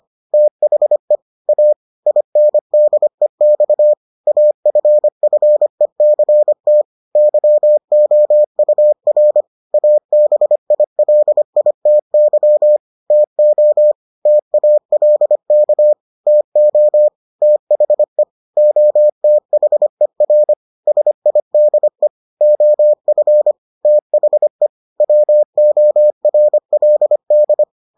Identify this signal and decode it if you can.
Morse code, heard as a keyed tone, about 25 wpm, 600 Hz. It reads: THE A INDEX AFFECT YOUR ABILITY TO TALK TO THE OTHER SIDE OF THE WORLD